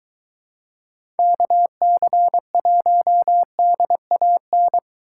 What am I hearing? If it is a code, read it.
Morse code, heard as a keyed tone, about 23 wpm, 700 Hz. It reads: KC1DAN